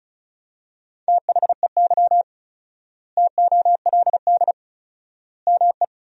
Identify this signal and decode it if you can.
Morse code, heard as a keyed tone, about 34 wpm, 700 Hz. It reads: THEY TOLD ME
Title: 4x Word-Spacing - Sentences from Top 400 Words - 35wpm